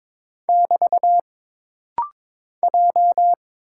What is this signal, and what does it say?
Morse code, heard as a keyed tone, about 22 wpm, 700 Hz. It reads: <BT> J
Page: This rapid-fire timing is geared to push your ICR (Instant Character Recognition) skills.